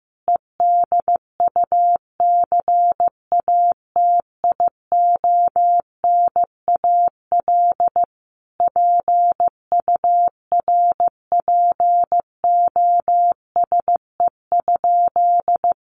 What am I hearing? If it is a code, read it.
Morse code, heard as a keyed tone, about 15 wpm, 700 Hz. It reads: EDUCATIONAL PURPOSE?
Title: Sentences from Top 1200 Words 15wpm